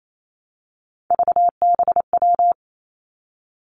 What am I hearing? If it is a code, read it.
Morse code, heard as a keyed tone, about 28 wpm, 700 Hz. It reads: VBW
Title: Three Challenging Letter-Number Combinations 28wpm